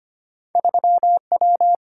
Morse code, heard as a keyed tone, about 25 wpm, 700 Hz. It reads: 3W
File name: Two Letter-Number Combinations - 25wpm